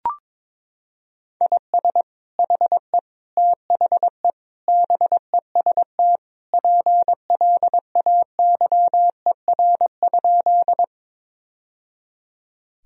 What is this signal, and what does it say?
Morse code, heard as a keyed tone, about 22 wpm, 700 Hz. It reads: IS HE THE BEST PLAYER?
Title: Sentences from Top 1000 Words - Review Part 4 - 22wpm